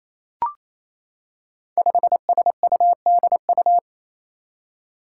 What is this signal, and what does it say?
Morse code, heard as a keyed tone, about 27 wpm, 700 Hz. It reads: 5SUDU
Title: Five Challenging Letter-Number Combinations 28wpm